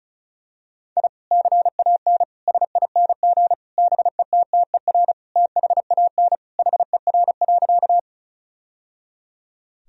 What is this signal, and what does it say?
Morse code, heard as a keyed tone, about 35 wpm, 700 Hz. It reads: I CAN SING BETTER THAN HER.
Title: Sentences from Top 1000 Words - Review Part 2 - 35wpm